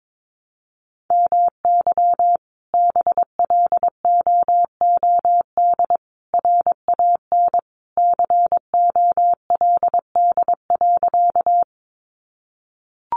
Morse code, heard as a keyed tone, about 22 wpm, 700 Hz. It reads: MY BLOOD RAN COLD.
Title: Sentences from Top 1100 Words 22wpm